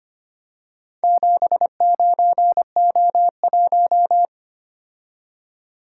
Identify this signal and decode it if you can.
Morse code, heard as a keyed tone, about 25 wpm, 700 Hz. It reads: 79O1
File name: practice / 25wpm / Four Letter-Number Combinations - 25wpm